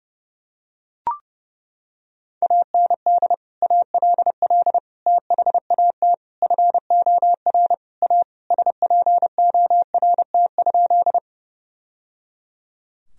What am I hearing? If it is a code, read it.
Morse code, heard as a keyed tone, about 28 wpm, 700 Hz. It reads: AND ALL THAT FOR A SPORT?
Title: Sentences from Top 1000 Words - Review Part 4 - 30wpm